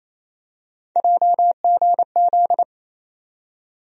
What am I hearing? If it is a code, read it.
Morse code, heard as a keyed tone, about 28 wpm, 700 Hz. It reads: JGZ
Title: USA Call Sign Postfixes - 3 Letters 28wpm